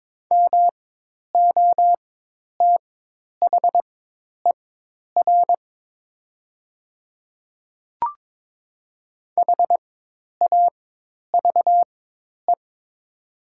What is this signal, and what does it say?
Morse code, heard as a keyed tone, about 22 wpm, spaced as for 10 wpm, 700 Hz. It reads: MOTHER HAVE
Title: 4x Character Spacing - Top 300 Words - 22wpm